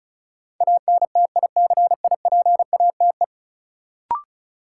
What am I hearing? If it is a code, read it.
Morse code, heard as a keyed tone, about 36 wpm, 700 Hz. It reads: ANTICIPATE
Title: Sentences from Top 1500 Words - New Word Drill - 35wpm